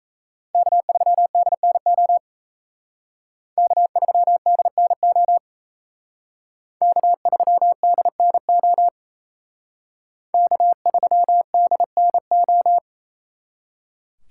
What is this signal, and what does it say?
Morse code, heard as a keyed tone, about 36 wpm, 700 Hz. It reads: K3DNO K3DNO K3DNO K3DNO